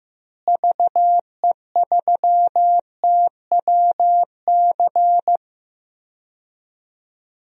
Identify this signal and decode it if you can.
Morse code, heard as a keyed tone, about 15 wpm, 700 Hz. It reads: VE3TWC
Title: DXCC-100 Call Signs - 6 Characters - Repeated 3x - 15wpm